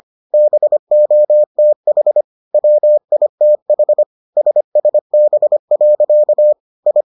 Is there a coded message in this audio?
Yes: Morse code from 0.3 to 7.0 s, heard as a keyed tone, about 25 wpm, 600 Hz. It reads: BOTH WITH SSB. I